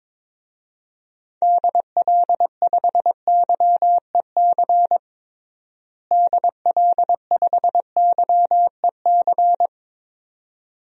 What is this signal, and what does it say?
Morse code, heard as a keyed tone, about 22 wpm, 700 Hz. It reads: DL5YEC DL5YEC
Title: DXCC-100 Call Signs - 6 Characters - Repeated 2x - 22wpm